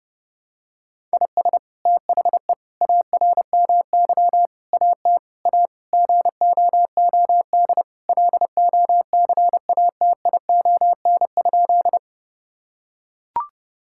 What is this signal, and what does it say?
Morse code, heard as a keyed tone, about 32 wpm, 700 Hz. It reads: IS THE ARMY AT A GOOD LOCATION?
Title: Sentences from Top 1000 Words - Review Part 6 - 30wpm